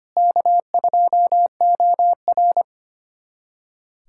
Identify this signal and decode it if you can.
Morse code, heard as a keyed tone, about 25 wpm, 700 Hz. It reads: K2OR
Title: USA Callsign DE Callsign 25wpm